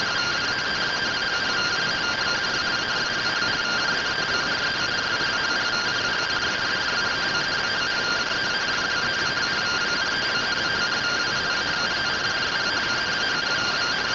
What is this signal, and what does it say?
Two-tone teleprinter signal, about 170 Hz shift.
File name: findings / MR CORONAL fr1 -131.gen-37 rtty rx